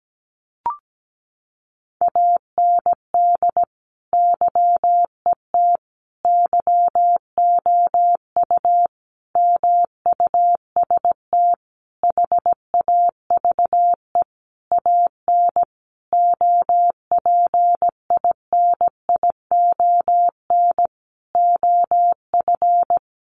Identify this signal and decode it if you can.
Morse code, heard as a keyed tone, about 17 wpm, 700 Hz. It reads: AND YET YOU MUST HAVE AN OPINION OF
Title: Sentences from Top 1300 Words 17wpm